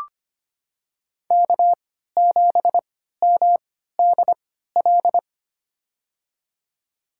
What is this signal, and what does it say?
Morse code, heard as a keyed tone, about 25 wpm, spaced as for 14 wpm, 700 Hz. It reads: K7MDL
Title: USA Call Signs - 5 Characters - 3x Character Spacing - 25wpm